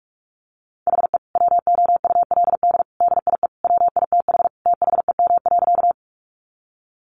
Morse code, heard as a keyed tone, about 44 wpm, 700 Hz. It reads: HE WOULD DIE WITH THEM.
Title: Sentences from Top 1300 Words 45wpm